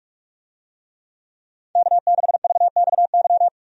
Morse code, heard as a keyed tone, about 46 wpm, 700 Hz. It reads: K6VXY
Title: USA Call Signs - 5 Characters - 45wpm